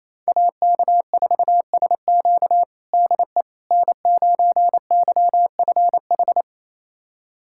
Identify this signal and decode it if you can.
Morse code, heard as a keyed tone, about 28 wpm, 700 Hz. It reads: AK4SQ DE N9YFH
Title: USA Callsign DE Callsign - 5 Characters - 28wpm